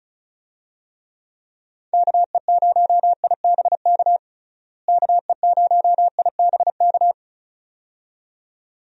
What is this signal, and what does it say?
Morse code, heard as a keyed tone, about 34 wpm, 700 Hz. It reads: KE0IBK KE0IBK
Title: USA Call Signs - 6 Characters - Repeated 2x - 35wpm